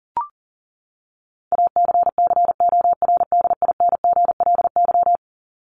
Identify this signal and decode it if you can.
Morse code, heard as a keyed tone, about 40 wpm, 700 Hz. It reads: ACCORDINGLY
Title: LY Suffix - Top 100 Words with Base Words 40wpm